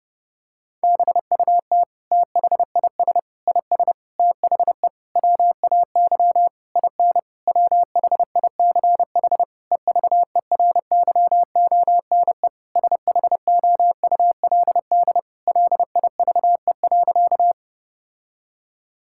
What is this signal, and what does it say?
Morse code, heard as a keyed tone, about 32 wpm, 700 Hz. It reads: BUT THIS IS THE WAY IN WHICH EVERYONE SHOULD LIVE.
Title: Sentences from Top 1000 Words - Review Part 5 - 30wpm